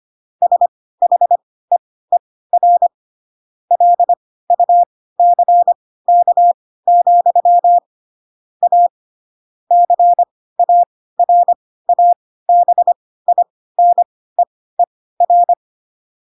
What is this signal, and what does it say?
Morse code, heard as a keyed tone, about 25 wpm, spaced as for 16 wpm, 700 Hz. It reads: SHEER LUCK, A CARABINEER
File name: Pinocchio / Pinocchio ch3 10 25 ik1hge